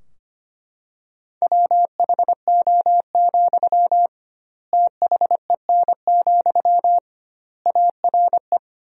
Morse code, heard as a keyed tone, about 25 wpm, 700 Hz. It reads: WHO, THEN, ARE
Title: 2x Word-Spacing - Sentences from Top 100 Words - 25wpm